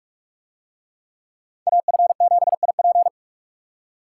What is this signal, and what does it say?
Morse code, heard as a keyed tone, about 44 wpm, 700 Hz. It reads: AF7IP